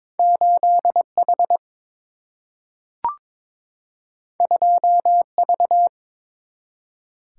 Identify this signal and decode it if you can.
Morse code, heard as a keyed tone, about 21 wpm, 700 Hz. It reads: 8H 2V
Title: Single-Letter Single-Number 22wpm